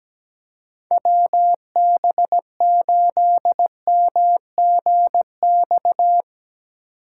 Morse code, heard as a keyed tone, about 17 wpm, 700 Hz. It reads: WB8MGX
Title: USA Call Signs - 6 Characters - 8x Character Spacing - 17wpm